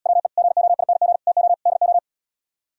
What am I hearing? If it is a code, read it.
Morse code, heard as a keyed tone, about 77 wpm, 700 Hz. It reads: WE GOING TO DO